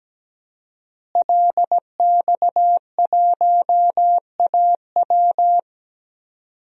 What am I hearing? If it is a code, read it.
Morse code, heard as a keyed tone, about 17 wpm, 700 Hz. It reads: LX1AW
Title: DXCC-100 Call Signs - 5 Characters - Repeated 3x - 17wpm